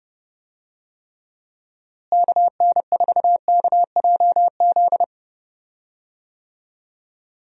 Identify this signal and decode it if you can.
Morse code, heard as a keyed tone, about 30 wpm, 700 Hz. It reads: KN4KJZ
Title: USA Call Signs - 6 Characters - Repeated 2x - 30wpm